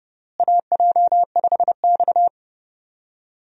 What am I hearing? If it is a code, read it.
Morse code, heard as a keyed tone, about 30 wpm, 700 Hz. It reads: AJ5X
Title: USA Call Signs - 4 Characters - Repeated 2x - 30wpm